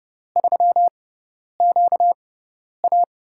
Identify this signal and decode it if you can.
Morse code, heard as a keyed tone, about 30 wpm, spaced as for 10 wpm, 700 Hz. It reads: 3QA
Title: USA Call Signs - 4 Characters - 6x Character Spacing - 30wpm